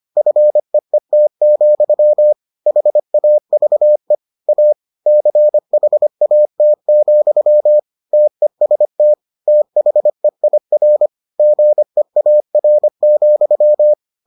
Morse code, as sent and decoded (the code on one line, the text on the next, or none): ..-. . - --..-- / .... .- ...- . / .- / -.-. .... .- - --..-- / - . ... - / - .... .. .-. / --. .- .-. --..--
FEET, HAVE A CHAT, TEST THEIR GEAR,